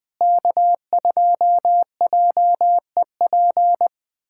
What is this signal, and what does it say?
Morse code, heard as a keyed tone, about 20 wpm, 700 Hz. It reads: K2JEP
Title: USA Callsign DE Callsign - 5 Characters - 20wpm